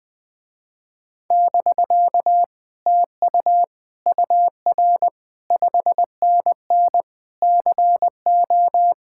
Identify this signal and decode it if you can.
Morse code, heard as a keyed tone, about 20 wpm, 700 Hz. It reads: <BK> TU UR 5NN CO
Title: Speed-Racing Common QSO Phrases Encore - 20wpm